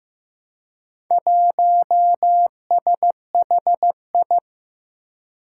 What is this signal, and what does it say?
Morse code, heard as a keyed tone, about 15 wpm, 700 Hz. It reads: 1SHI
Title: Four Challenging Letter-Number Combinations 15wpm